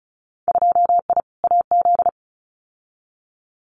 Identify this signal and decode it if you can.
Morse code, heard as a keyed tone, about 34 wpm, 700 Hz. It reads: 2I AZ
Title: Winter Field Day Exchanges 35wpm